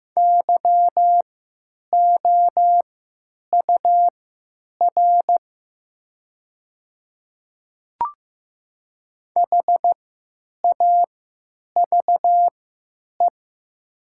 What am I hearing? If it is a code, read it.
Morse code, heard as a keyed tone, about 15 wpm, spaced as for 9 wpm, 700 Hz. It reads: YOUR HAVE
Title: 3x Character Spacing - Top 100 Words - 15wpm